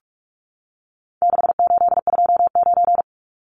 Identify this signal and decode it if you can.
Morse code, heard as a keyed tone, about 44 wpm, 700 Hz. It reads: 6829
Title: Numbers - 4-Digit 45wpm